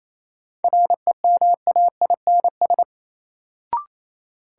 Morse code, heard as a keyed tone, about 28 wpm, 700 Hz. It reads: REMAINS
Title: Sets of 5 Words Encore - New Word Drill - 28wpm